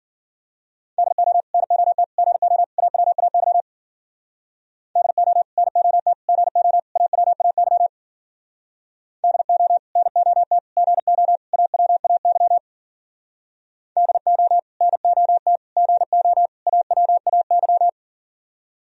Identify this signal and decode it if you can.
Morse code, heard as a keyed tone, about 50 wpm, 700 Hz. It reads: DO NOT GO AWAY DO NOT GO AWAY DO NOT GO AWAY DO NOT GO AWAY